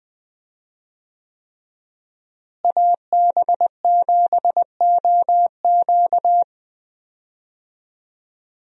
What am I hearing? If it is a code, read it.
Morse code, heard as a keyed tone, about 20 wpm, 700 Hz. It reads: AB7OQ
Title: USA Call Signs - 5 Characters - Repeated 3x - 20wpm